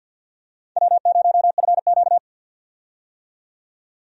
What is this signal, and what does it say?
Morse code, heard as a keyed tone, about 50 wpm, 700 Hz. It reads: W0FQ
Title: USA Call Signs - 4 Characters - 50wpm